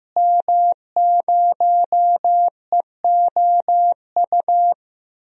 Morse code, heard as a keyed tone, about 15 wpm, 700 Hz. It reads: M0EOU
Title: DXCC-100 Call Signs - 5 Characters - Repeated 3x - 15wpm